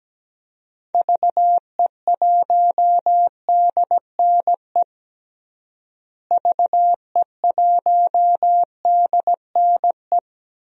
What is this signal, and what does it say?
Morse code, heard as a keyed tone, about 17 wpm, 700 Hz. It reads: VE1DNE VE1DNE
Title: DXCC-100 Call Signs - 6 Characters - Repeated 2x - 17wpm